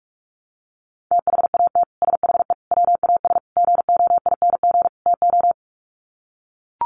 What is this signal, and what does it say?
Morse code, heard as a keyed tone, about 44 wpm, 690 Hz. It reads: THAT SHE WAS GOING TO
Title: Sets of 5 Words Encore2 45wpm